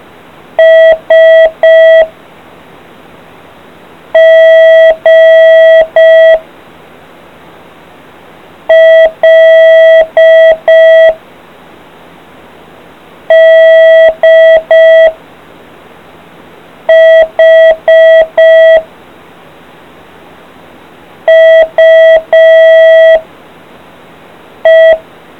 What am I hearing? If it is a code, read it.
Morse code, heard as a keyed tone, about 5 wpm, 650 Hz. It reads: S G L D H U E